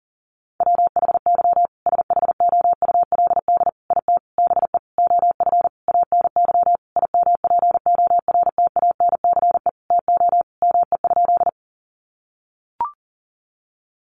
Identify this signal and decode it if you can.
Morse code, heard as a keyed tone, about 40 wpm, 700 Hz. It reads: WHY SHOULD IT BE OF ANY IMPORTANCE TO ME?
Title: Sentences from Top 1200 Words 40wpm